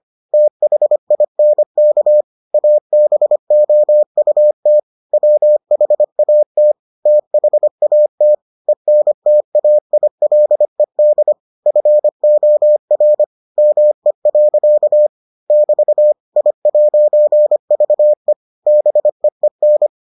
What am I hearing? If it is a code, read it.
Morse code, heard as a keyed tone, about 25 wpm, 600 Hz. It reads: THINK ABOUT WHAT THAT ENTAILED FOR ME. <BT> I'VE BEEN